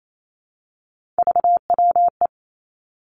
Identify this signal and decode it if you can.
Morse code, heard as a keyed tone, about 28 wpm, 700 Hz. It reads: VWE
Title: Three Challenging Letter-Number Combinations 28wpm